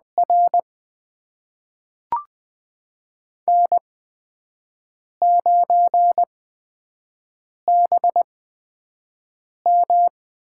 Morse code, heard as a keyed tone, about 20 wpm, spaced as for 5 wpm, 700 Hz. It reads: R N9BM
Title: USA Call Signs - 4 Characters - 8x Character Spacing - 20wpm